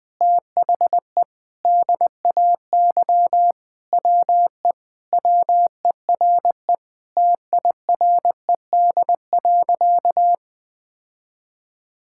Morse code, heard as a keyed tone, about 20 wpm, 700 Hz. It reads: THE DAY WE WERE TIRED.
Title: Sentences with 5-Word Phrases - Spoken Beginning - 20wpm